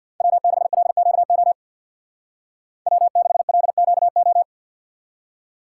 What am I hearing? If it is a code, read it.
Morse code, heard as a keyed tone, about 53 wpm, 700 Hz. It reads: W6LQO W6LQO
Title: Speed Racing - USA Call Signs 50wpm